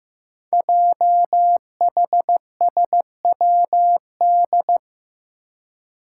Morse code, heard as a keyed tone, about 15 wpm, 700 Hz. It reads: JHSWD
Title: Code Groups - 5 Letters - 15wpm